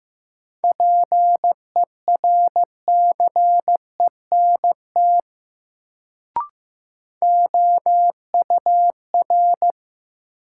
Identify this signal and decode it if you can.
Morse code, heard as a keyed tone, about 15 wpm, 700 Hz. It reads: PERCENT OUR
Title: Sets of 2 Words - New Word Drill - 15wpm